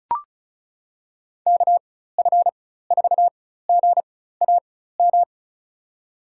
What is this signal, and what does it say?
Morse code, heard as a keyed tone, about 34 wpm, spaced as for 16 wpm, 700 Hz. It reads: KF4GAM
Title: USA Call Signs - 4x Character Spacing - 35wpm